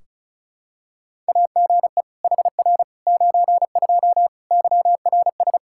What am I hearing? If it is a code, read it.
Morse code, heard as a keyed tone, about 36 wpm, 700 Hz. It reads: AGE HR 92 YRS
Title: Practice QSO - Exchange 3 - More Info - 35wpm